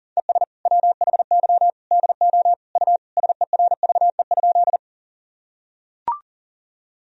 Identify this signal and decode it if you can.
Morse code, heard as a keyed tone, about 40 wpm, 700 Hz. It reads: ES WHY DO U SERVE?
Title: Sentences from Top 400 Words and Abbreviations 40wpm